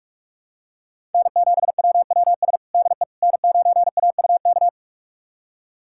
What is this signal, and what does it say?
Morse code, heard as a keyed tone, about 41 wpm, 700 Hz. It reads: N7WWS DE N9AUK
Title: USA Callsign DE Callsign - 5 Characters - 45wpm